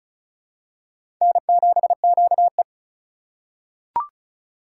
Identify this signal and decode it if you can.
Morse code, heard as a keyed tone, about 36 wpm, 700 Hz. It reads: N7QE
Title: USA Call Signs - 4 Characters - 3x Character Spacing - 35wpm